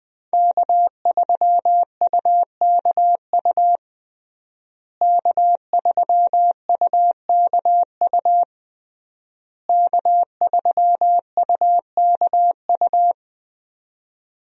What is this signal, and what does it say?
Morse code, heard as a keyed tone, about 20 wpm, 700 Hz. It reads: K3UKU K3UKU K3UKU
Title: USA Call Signs - 5 Characters - Repeated 3x - 20wpm